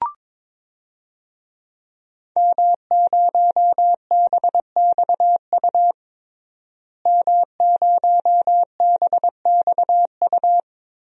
Morse code, heard as a keyed tone, about 21 wpm, 700 Hz. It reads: M0BXU M0BXU
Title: DXCC-100 Call Signs - 5 Characters - Repeated 3x - 22wpm